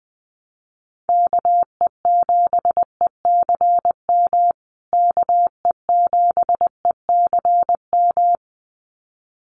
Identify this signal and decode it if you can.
Morse code, heard as a keyed tone, about 20 wpm, 700 Hz. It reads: KE7ECM KE7ECM
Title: USA Call Signs Repeated 2x 20wpm